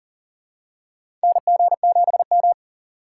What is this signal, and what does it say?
Morse code, heard as a keyed tone, about 40 wpm, 690 Hz. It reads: NG7M
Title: USA Call Signs - 2x Character Spacing - 40wpm